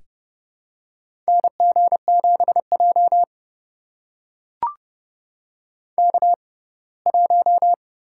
Morse code, heard as a keyed tone, about 32 wpm, spaced as for 21 wpm, 700 Hz. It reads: NG7J K1
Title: USA Call Signs - 4 Characters - 6x Character Spacing - 30wpm